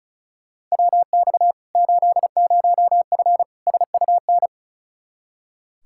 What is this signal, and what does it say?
Morse code, heard as a keyed tone, about 36 wpm, 700 Hz. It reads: WX 80F SUN
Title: Practice QSO - Exchange 2 - Info - 35wpm